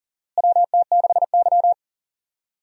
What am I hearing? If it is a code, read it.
Morse code, heard as a keyed tone, about 40 wpm, 700 Hz. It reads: WT6Y
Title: USA Call Signs - 4 Characters - Repeated 2x - 40wpm